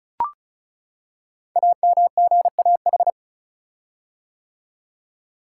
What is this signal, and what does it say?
Morse code, heard as a keyed tone, about 36 wpm, 690 Hz. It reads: AMGAH
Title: Code Groups - 5 Letters - 35wpm